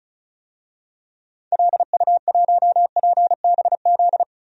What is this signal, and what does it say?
Morse code, heard as a keyed tone, about 34 wpm, 700 Hz. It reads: LU1PBZ
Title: DXCC-100 Call Signs - 6 Characters - Repeated 2x - 35wpm